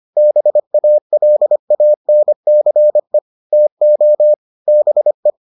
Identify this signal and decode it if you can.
Morse code, heard as a keyed tone, about 25 wpm, 600 Hz. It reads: BALANCE TO BE